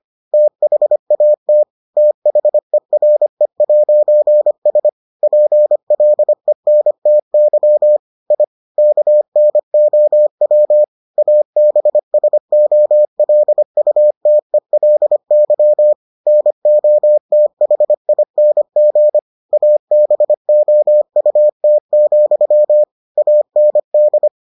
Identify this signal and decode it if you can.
Morse code, heard as a keyed tone, about 25 wpm, 600 Hz. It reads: THAT THERE'S PLENTY I KNOW ABSOLUTELY NOTHING ABOUT, AND